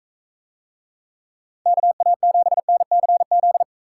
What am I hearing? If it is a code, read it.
Morse code, heard as a keyed tone, about 39 wpm, 700 Hz. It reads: KA7NCZ